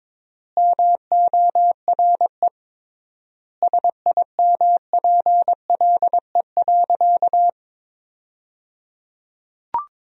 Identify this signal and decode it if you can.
Morse code, heard as a keyed tone, about 22 wpm, 700 Hz. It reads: MORE SIMPLE.
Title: 3x Word-Spacing - Sentences from Top 500 Words - 22wpm